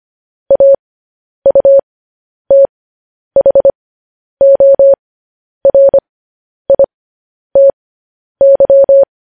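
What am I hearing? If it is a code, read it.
Morse code, heard as a keyed tone, about 25 wpm, spaced as for 10 wpm, 550 Hz. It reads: AUTHORITY
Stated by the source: Morse code practice sending news headlines at 10 wpm.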